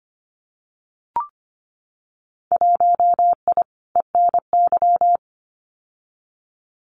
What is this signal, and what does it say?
Morse code, heard as a keyed tone, about 25 wpm, 700 Hz. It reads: 1I ENY
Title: Winter Field Day Exchanges Encore 25wpm